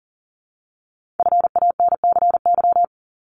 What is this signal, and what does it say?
Morse code, heard as a keyed tone, about 40 wpm, 700 Hz. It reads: FANCY
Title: Multi-letter Phonemes 40wpm